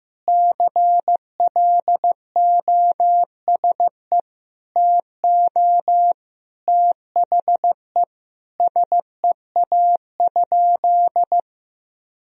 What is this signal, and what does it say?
Morse code, heard as a keyed tone, about 16 wpm, 700 Hz. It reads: CLOSE TO THE SEA?
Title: Sentences from Top 1000 Words - Review Part 1 - 15wpm